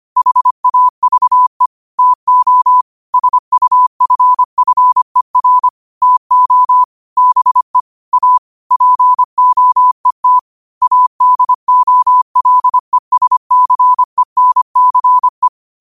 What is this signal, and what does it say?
Morse code, heard as a keyed tone, about 25 wpm, 1000 Hz. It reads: HAVE TO SUFFER TO BE A POET ADOLESCENCE
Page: New quotes every day in morse code at 25 Words per minute.